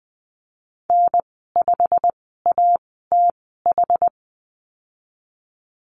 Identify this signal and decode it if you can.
Morse code, heard as a keyed tone, about 20 wpm, spaced as for 15 wpm, 700 Hz. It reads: N5ATH
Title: 122 Lesson 20g - Number 3 - Call Signs 20wpm